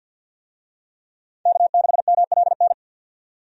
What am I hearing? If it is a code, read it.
Morse code, heard as a keyed tone, about 48 wpm, 700 Hz. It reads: K6MLN
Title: USA Call Signs - 5 Characters - 50wpm